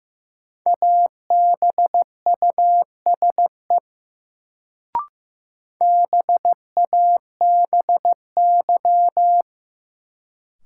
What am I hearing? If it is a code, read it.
Morse code, heard as a keyed tone, about 16 wpm, 700 Hz. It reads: ABUSE BABY
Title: Sentences from Top 1300 Words - New Word Drill - 15wpm